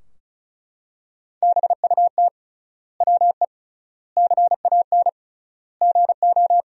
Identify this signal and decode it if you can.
Morse code, heard as a keyed tone, about 34 wpm, 700 Hz. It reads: BUT WE CAN GO
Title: 3x Word-Spacing - Sentences from Top 100 Words - 35wpm